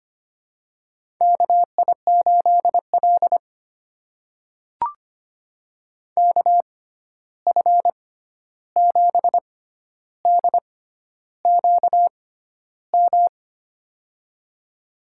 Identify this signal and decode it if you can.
Morse code, heard as a keyed tone, about 25 wpm, spaced as for 11 wpm, 700 Hz. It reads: KI8L KF7DQM
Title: USA Call Signs - 6x Character Spacing - 25wpm